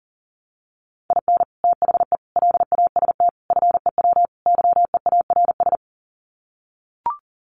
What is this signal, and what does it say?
Morse code, heard as a keyed tone, about 40 wpm, 700 Hz. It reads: IN THE LAST FEW YEARS
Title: Sets of 5 Words 40wpm